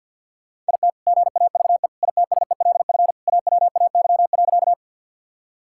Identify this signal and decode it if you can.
Morse code, heard as a keyed tone, about 46 wpm, 700 Hz. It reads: IT GAVE ITSELF AWAY.
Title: Sentences from Top 1000 Words - Review Part 5 - 50wpm